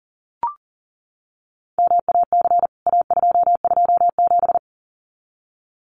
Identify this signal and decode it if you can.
Morse code, heard as a keyed tone, about 40 wpm, 700 Hz. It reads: MAC A227
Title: CWT Exchange 40wpm